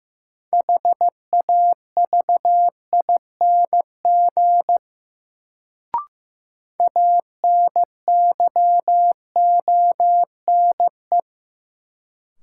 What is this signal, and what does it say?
Morse code, heard as a keyed tone, about 15 wpm, 700 Hz. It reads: HAVING ANYONE
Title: Sets of 5 Words Encore - New Word Drill - 15wpm